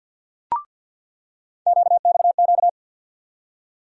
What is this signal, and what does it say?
Morse code, heard as a keyed tone, about 49 wpm, 700 Hz. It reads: QXQ
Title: USA Call Sign Postfixes - 3 Letters 50wpm